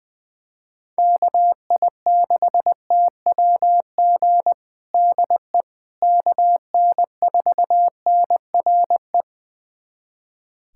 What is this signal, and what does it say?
Morse code, heard as a keyed tone, about 20 wpm, 700 Hz. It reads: KI6TWG DE KN4NRE
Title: USA Callsign DE Callsign - 6 Characters - 20wpm